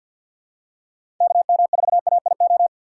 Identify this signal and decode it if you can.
Morse code, heard as a keyed tone, about 51 wpm, 700 Hz. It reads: KM4AIO
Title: USA Call Signs - 4x Character Spacing - 50wpm